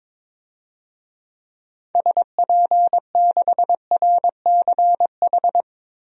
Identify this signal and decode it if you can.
Morse code, heard as a keyed tone, about 22 wpm, 700 Hz. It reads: SP6RCH
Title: DXCC-100 Call Signs - 6 Characters - Repeated 3x - 22wpm